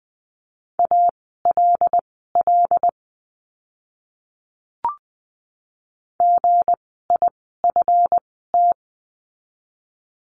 Morse code, heard as a keyed tone, about 20 wpm, spaced as for 15 wpm, 700 Hz. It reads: ALL GIFT
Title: 160 Lesson 26f - Letter g - Words 20wpm